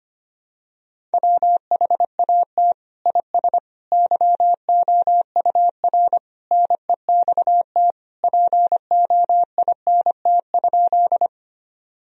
Morse code, heard as a keyed tone, about 25 wpm, 700 Hz. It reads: WHAT IS YOUR NEXT POINT?
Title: Sentences from Top 1000 Words - Review Part 1 - 25wpm